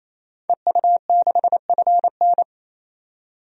Morse code, heard as a keyed tone, about 27 wpm, 700 Hz. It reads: EU6FN
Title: DXCC-100 Call Signs - 5 Characters - 28wpm